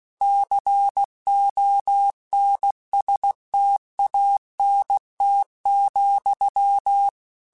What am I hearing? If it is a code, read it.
Morse code, heard as a keyed tone, about 16 wpm, 800 Hz. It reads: CONSTANT,
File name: Technical Subjects / Technical Subjects-mp3s / XC @15